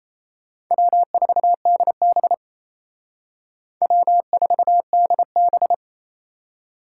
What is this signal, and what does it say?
Morse code, heard as a keyed tone, about 35 wpm, 700 Hz. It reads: W4DB W4DB
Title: Speed Racing - USA 1x2 Call Signs 28wpm